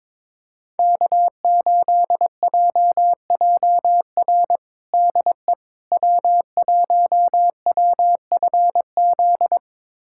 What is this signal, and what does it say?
Morse code, heard as a keyed tone, about 22 wpm, 700 Hz. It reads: K8JJR DE W1WFZ
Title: USA Callsign DE Callsign - 5 Characters - 22wpm